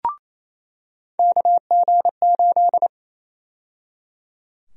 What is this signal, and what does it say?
Morse code, heard as a keyed tone, about 28 wpm, 700 Hz. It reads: KG8
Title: USA Call Sign Prefixes - 28wpm